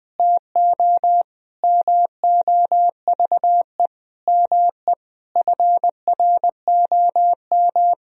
Morse code, heard as a keyed tone, about 21 wpm, 700 Hz. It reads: TO MOVE ME FROM
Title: Sentences from Top 1000 Words - Review Part 1 - 20wpm